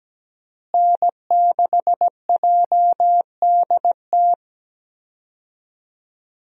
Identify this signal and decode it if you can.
Morse code, heard as a keyed tone, about 17 wpm, 700 Hz. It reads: N6JDT
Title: USA Call Signs - 5 Characters - Repeated 3x - 17wpm